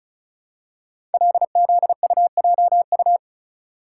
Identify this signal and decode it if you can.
Morse code, heard as a keyed tone, about 36 wpm, 700 Hz. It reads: LZUJU
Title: Code Groups - 5 Letters - 35wpm